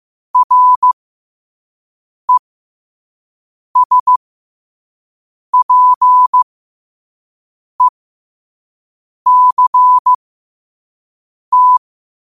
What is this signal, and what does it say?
Morse code, heard as a keyed tone, about 15 wpm, spaced as for 5 wpm, 1000 Hz. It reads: RESPECT
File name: qotdfiles / 5wpm